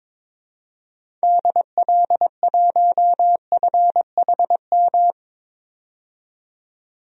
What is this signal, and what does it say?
Morse code, heard as a keyed tone, about 21 wpm, 700 Hz. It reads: DL1FHM
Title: DXCC-100 Call Signs - 6 Characters - 22wpm